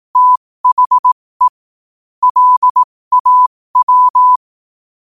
Morse code, heard as a keyed tone, about 18 wpm, spaced as for 15 wpm, 1000 Hz. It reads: THE LAW
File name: qotdfiles / 15wpm